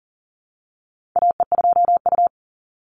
Morse code, heard as a keyed tone, about 38 wpm, 700 Hz. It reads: AE2U
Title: Call Signs - 40wpm